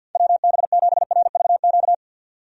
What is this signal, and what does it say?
Morse code, heard as a keyed tone, about 52 wpm, 700 Hz. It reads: WB7RVQ
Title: USA Call Signs - 6 Characters - 50wpm